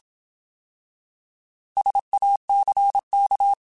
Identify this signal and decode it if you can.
Morse code, heard as a keyed tone, about 27 wpm, 800 Hz. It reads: SACK
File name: Alphabetical / Alphabetical mp3s / Ss @25